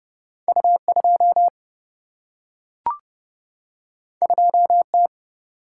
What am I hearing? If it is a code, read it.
Morse code, heard as a keyed tone, about 30 wpm, 700 Hz. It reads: U2 2T
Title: Single-Letter Single-Number 30wpm